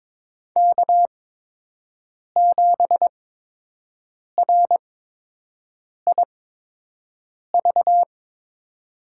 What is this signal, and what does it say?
Morse code, heard as a keyed tone, about 22 wpm, spaced as for 6 wpm, 700 Hz. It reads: K7RIV
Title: USA Call Signs - 5 Characters - 8x Character Spacing - 22wpm